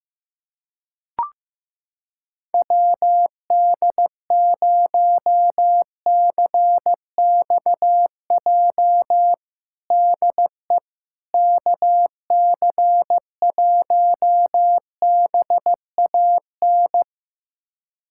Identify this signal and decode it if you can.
Morse code, heard as a keyed tone, about 16 wpm, 700 Hz. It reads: WD0CXJ DE KC1BAN
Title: USA Callsign DE Callsign - 6 Characters - 15wpm